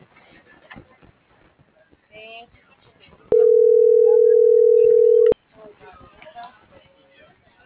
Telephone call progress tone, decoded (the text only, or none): ringback tone